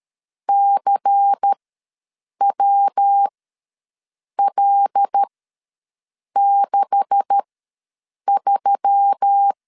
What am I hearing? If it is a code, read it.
Morse code, heard as a keyed tone, about 13 wpm, spaced as for 6 wpm, 800 Hz. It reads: CWL63
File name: LEZIONI / 9 286 ws 12wpm 3351